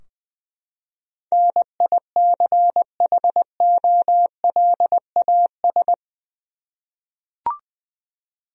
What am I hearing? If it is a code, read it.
Morse code, heard as a keyed tone, about 20 wpm, 700 Hz. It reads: NICHOLAS
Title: USA Common Names - 20wpm